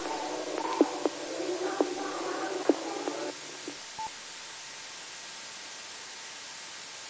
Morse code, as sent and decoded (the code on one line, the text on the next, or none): none